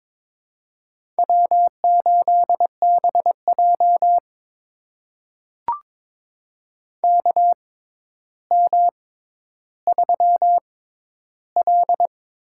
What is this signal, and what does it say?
Morse code, heard as a keyed tone, about 22 wpm, spaced as for 11 wpm, 700 Hz. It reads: W8BJ KM3L
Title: USA Call Signs - 4 Characters - 6x Character Spacing - 22wpm